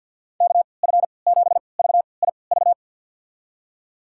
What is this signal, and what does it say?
Morse code, heard as a keyed tone, about 50 wpm, spaced as for 28 wpm, 700 Hz. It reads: KF7VIV
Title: USA Call Signs - 6 Characters - 3x Character Spacing - 50wpm